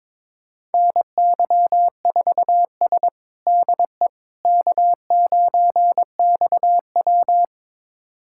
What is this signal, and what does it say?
Morse code, heard as a keyed tone, about 22 wpm, 700 Hz. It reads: NY4S DE K9XW
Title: USA Callsign DE Callsign - 4 Characters - 22wpm